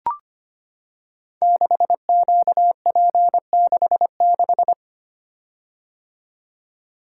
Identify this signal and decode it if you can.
Morse code, heard as a keyed tone, about 25 wpm, 700 Hz. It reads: 6QP66
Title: Code Groups - 5 Letter:Numbers - 25wpm